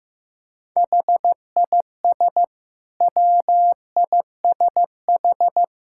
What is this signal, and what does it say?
Morse code, heard as a keyed tone, about 15 wpm, 700 Hz. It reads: HIS WISH
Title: Sentences from Top 1600 Words 15wpm